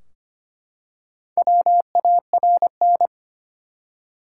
Morse code, heard as a keyed tone, about 25 wpm, 700 Hz. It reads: WARN
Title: Four Letter Words - 25wpm